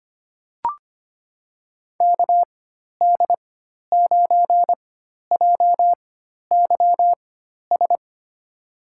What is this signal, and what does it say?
Morse code, heard as a keyed tone, about 25 wpm, spaced as for 12 wpm, 700 Hz. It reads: KD9JYS
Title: USA Call Signs - 6 Characters - 4x Character Spacing - 25wpm